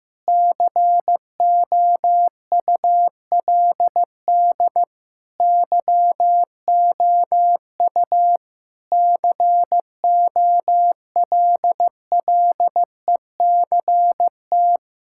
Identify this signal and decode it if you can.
Morse code, heard as a keyed tone, about 15 wpm, 700 Hz. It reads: COULD YOU COLLECT
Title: Sentences from Top 1600 Words 15wpm